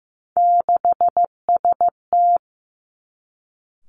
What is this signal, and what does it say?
Morse code, heard as a keyed tone, about 15 wpm, 700 Hz. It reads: 6ST
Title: Three Challenging Letter-Number Combinations 15wpm